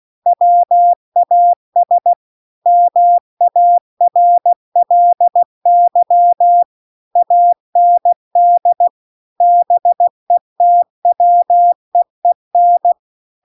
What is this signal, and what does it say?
Morse code, heard as a keyed tone, about 16 wpm, 700 Hz. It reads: WAS MARLY AND BETWEEN
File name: CW-The-Big-Twoheaded-River-2-16WPM